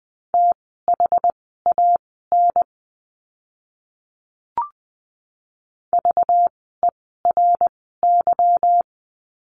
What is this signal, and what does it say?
Morse code, heard as a keyed tone, about 20 wpm, spaced as for 14 wpm, 700 Hz. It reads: THAN VERY
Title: 237 Lesson 38f - Letter q - Words 20wpm